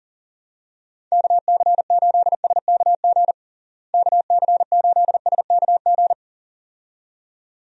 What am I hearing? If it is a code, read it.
Morse code, heard as a keyed tone, about 40 wpm, 700 Hz. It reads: KC8SKG KC8SKG
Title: USA Call Signs - 6 Characters - Repeated 2x - 40wpm